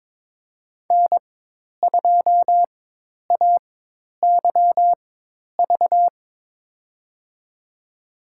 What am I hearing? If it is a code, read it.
Morse code, heard as a keyed tone, about 22 wpm, spaced as for 10 wpm, 700 Hz. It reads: N2AYV
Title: USA Call Signs - 4x Character Spacing - 22wpm